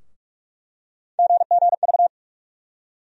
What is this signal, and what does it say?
Morse code, heard as a keyed tone, about 46 wpm, 700 Hz. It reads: GGV
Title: USA Call Sign Postfixes - 3 Letters 45wpm